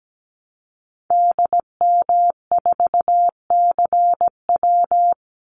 Morse code, heard as a keyed tone, about 17 wpm, 700 Hz. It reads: DM4CW
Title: Call Signs - 17wpm